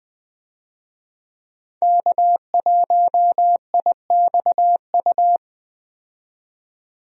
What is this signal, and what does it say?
Morse code, heard as a keyed tone, about 20 wpm, 700 Hz. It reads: K1IXU
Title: USA Call Signs - 5 Characters - 20wpm